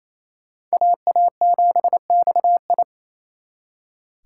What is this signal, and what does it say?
Morse code, heard as a keyed tone, about 35 wpm, 700 Hz. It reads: AA7XI
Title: Speed Racing - USA 2x2 Call Signs 28wpm